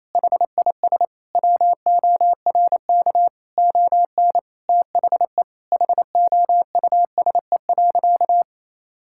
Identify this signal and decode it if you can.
Morse code, heard as a keyed tone, about 28 wpm, 700 Hz. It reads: HIS WORK ON THE HOUSE.
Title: Sentences from Top 1000 Words - Review Part 5 - 28wpm